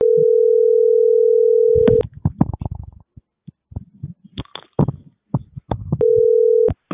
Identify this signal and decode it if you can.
Ringback tone (North America), 440 + 480 Hz.